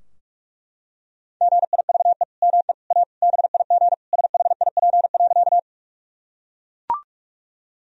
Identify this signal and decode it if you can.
Morse code, heard as a keyed tone, about 41 wpm, 700 Hz. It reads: GIVE ME A BIG SHIP.
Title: Sentences from Top 1000 Words - Review Part 2 - 45wpm